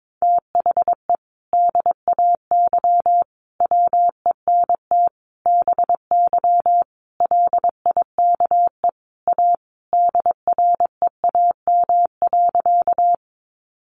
Morse code, heard as a keyed tone, about 22 wpm, 700 Hz. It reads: THE DAY WENT BY LIKE A DREAM.
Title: Sentences from Top 1500 Words 22wpm